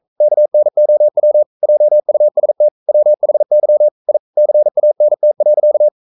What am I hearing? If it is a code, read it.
Morse code, heard as a keyed tone, about 42 wpm, 600 Hz. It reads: KNOW JUST WHY I CANT.
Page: Romanen Never af Ken Follet på engelsk.